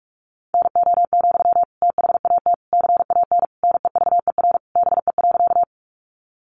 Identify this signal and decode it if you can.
Morse code, heard as a keyed tone, about 44 wpm, 700 Hz. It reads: NO, THAT CAN NEVER BE.
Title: Sentences from Top 300 Words 45wpm